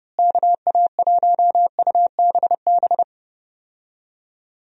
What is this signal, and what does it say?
Morse code, heard as a keyed tone, about 30 wpm, 700 Hz. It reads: KA1UBB
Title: USA Call Signs - 6 Characters - Repeated 2x - 30wpm